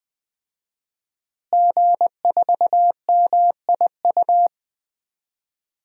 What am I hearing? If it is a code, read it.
Morse code, heard as a keyed tone, about 20 wpm, 700 Hz. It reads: G4MIU
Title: DXCC-100 Call Signs - 5 Characters - Repeated 2x - 20wpm